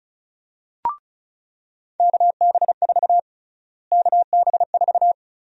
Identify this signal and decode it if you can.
Morse code, heard as a keyed tone, about 36 wpm, 700 Hz. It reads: KB4 KB4
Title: USA Call Sign Prefixes - Repeated 2x - 35wpm